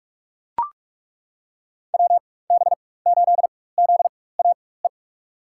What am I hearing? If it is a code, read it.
Morse code, heard as a keyed tone, about 42 wpm, spaced as for 20 wpm, 700 Hz. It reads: WB8ZAE
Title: USA Call Signs - 6 Characters - 4x Character Spacing - 45wpm